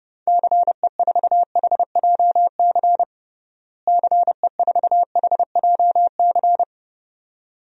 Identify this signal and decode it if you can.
Morse code, heard as a keyed tone, about 30 wpm, 700 Hz. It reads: CE4HJC CE4HJC
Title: DXCC-100 Call Signs - 6 Characters - Repeated 2x - 30wpm